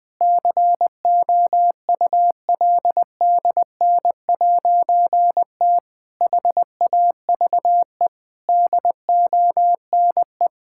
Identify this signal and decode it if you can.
Morse code, heard as a keyed tone, about 20 wpm, 700 Hz. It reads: COULDN'T HAVE DONE
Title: Difficult Sentences with Hints - 20wpm